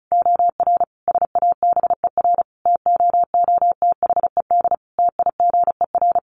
Morse code, heard as a keyed tone, about 34 wpm, 700 Hz. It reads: OR SABER TOOTHED TIGER
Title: A Sound Of Thunder 35wpm